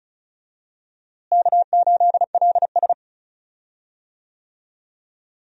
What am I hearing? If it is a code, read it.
Morse code, heard as a keyed tone, about 36 wpm, 700 Hz. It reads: K8LS
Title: USA Call Signs - 4 Characters - Repeated 2x - 35wpm